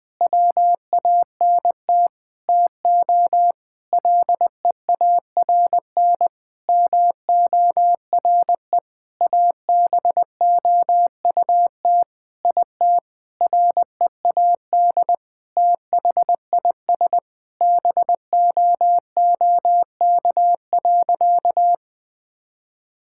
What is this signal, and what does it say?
Morse code, heard as a keyed tone, about 20 wpm, 700 Hz. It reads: WANT TO LEARN MORE ABOUT IT READ THIS BOOK.
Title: Sentences with 7-Word Phrases - 20wpm